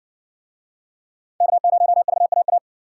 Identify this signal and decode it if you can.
Morse code, heard as a keyed tone, about 58 wpm, 700 Hz. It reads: K0VAA